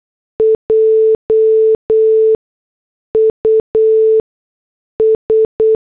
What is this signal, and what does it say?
Morse code, heard as a keyed tone, about 8 wpm, 440 Hz. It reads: J U S
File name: morse / headlines